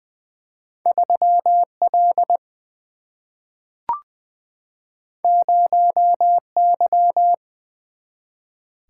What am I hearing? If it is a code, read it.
Morse code, heard as a keyed tone, about 20 wpm, 700 Hz. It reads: 3 L 0 Y
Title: Single-Letter Single-Number 20wpm